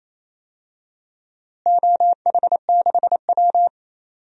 Morse code, heard as a keyed tone, about 28 wpm, 700 Hz. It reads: OH6W
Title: DXCC-100 Call Signs - 4 Characters - Repeated 2x - 28wpm